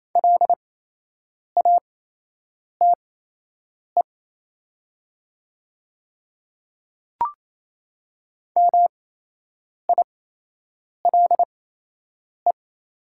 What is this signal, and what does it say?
Morse code, heard as a keyed tone, about 28 wpm, spaced as for 8 wpm, 700 Hz. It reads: LATE MILE
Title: 8x Character Spacing - Top 500 Words - 28wpm